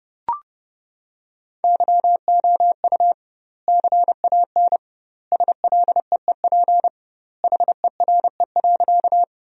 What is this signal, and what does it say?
Morse code, heard as a keyed tone, about 30 wpm, 700 Hz. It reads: YOU CAN SLEEP HERE.
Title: 2x Word-Spacing - Sentences from Top 500 Words - 30wpm